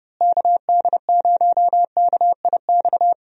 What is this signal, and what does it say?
Morse code, heard as a keyed tone, about 31 wpm, 700 Hz. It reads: KD0KIX
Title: USA Call Signs - 6 Characters - Repeated 3x - 30wpm